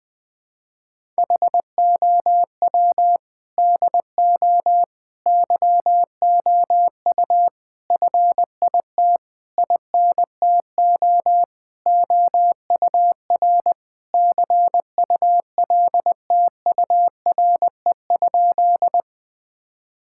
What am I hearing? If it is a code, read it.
Morse code, heard as a keyed tone, about 20 wpm, 700 Hz. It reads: HOW DO YOU FIT INTO OUR CULTURE?
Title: Sentences from Top 1000 Words - Review Part 6 - 20wpm